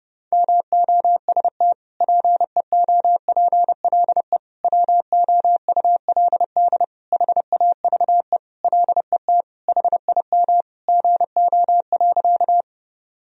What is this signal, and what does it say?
Morse code, heard as a keyed tone, about 30 wpm, 700 Hz. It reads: MOST PEOPLE WOULD HAVE LET HIM GO.
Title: Sentences from Top 1000 Words - Review Part 1 - 30wpm